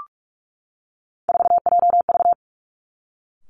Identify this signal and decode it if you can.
Morse code, heard as a keyed tone, about 46 wpm, 700 Hz. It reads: 4JV
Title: Three Challenging Letter-Number Combinations 45wpm